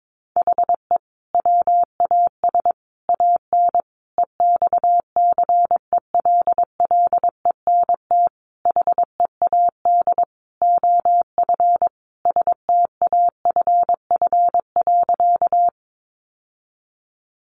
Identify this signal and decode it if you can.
Morse code, heard as a keyed tone, about 22 wpm, 700 Hz. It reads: HE WAS AN EXCELLENT HEAD OF STAFF.
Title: Sentences from Top 1200 Words 22wpm